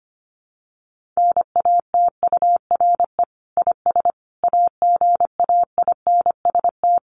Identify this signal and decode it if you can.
Morse code, heard as a keyed tone, about 25 wpm, 700 Hz. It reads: NATURE IS AGAINST
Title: Sentences from Top 1500 Words 25wpm